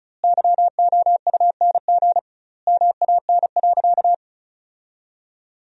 Morse code, heard as a keyed tone, about 36 wpm, 700 Hz. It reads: YOUNG MAN.
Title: 2x Word-Spacing - Sentences from Top 400 Words - 35wpm